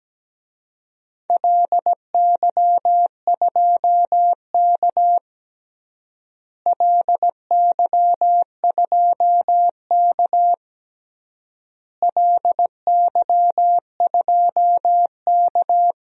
Morse code, heard as a keyed tone, about 17 wpm, 700 Hz. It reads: LY2K LY2K LY2K
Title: DXCC-100 Call Signs - 4 Characters - Repeated 3x - 17wpm